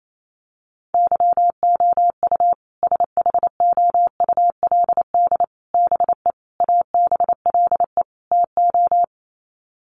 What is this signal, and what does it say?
Morse code, heard as a keyed tone, about 28 wpm, 700 Hz. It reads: YOU SHOULD BE ABLE TO
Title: Sets of 5 Words Encore 28wpm